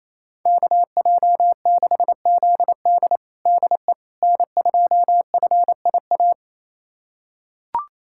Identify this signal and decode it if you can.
Morse code, heard as a keyed tone, about 28 wpm, 700 Hz. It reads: KJ6ZD DE N2FIA
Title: USA Callsign DE Callsign - 5 Characters - 28wpm